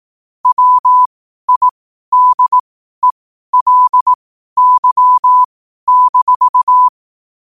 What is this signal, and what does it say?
Morse code, heard as a keyed tone, about 18 wpm, spaced as for 12 wpm, 1000 Hz. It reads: WIDELY-
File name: qotdfiles / 12wpm